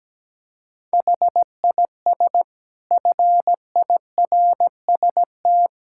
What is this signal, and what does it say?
Morse code, heard as a keyed tone, about 17 wpm, 700 Hz. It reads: HIS FIRST
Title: Sentences from Top 1600 Words 17wpm